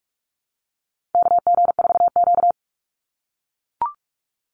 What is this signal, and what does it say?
Morse code, heard as a keyed tone, about 44 wpm, 700 Hz. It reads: KG4Q
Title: Call Signs - 45wpm